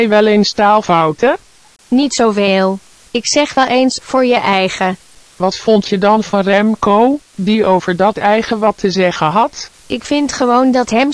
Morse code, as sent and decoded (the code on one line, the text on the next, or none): none